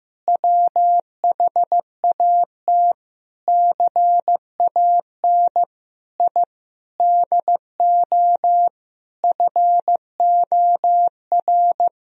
Morse code, heard as a keyed tone, about 15 wpm, 700 Hz. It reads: WHAT CAN I DO FOR
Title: Sentences from Top 1000 Words - Review Part 2 - 15wpm